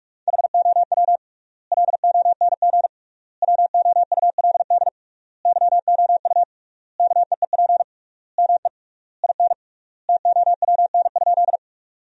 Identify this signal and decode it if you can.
Morse code, heard as a keyed tone, about 46 wpm, 700 Hz. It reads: HOW LONG WOULD YOU KEEP ME IN TOWN?
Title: 3x Word-Spacing - Sentences from Top 500 Words - 45wpm